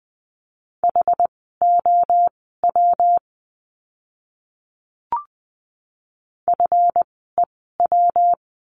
Morse code, heard as a keyed tone, about 20 wpm, spaced as for 14 wpm, 700 Hz. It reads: HOW FEW
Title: 206 Lesson 33f - Letter j - Words 20wpm